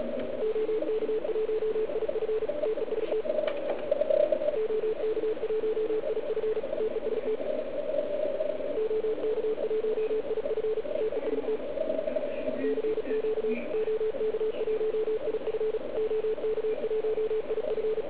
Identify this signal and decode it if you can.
Morse code, heard as a keyed tone, about 36 wpm, 440 Hz, with radio noise. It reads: OK1IF TEST OK1IF TEST OK1IF TEST CQ OK1IF OK1IF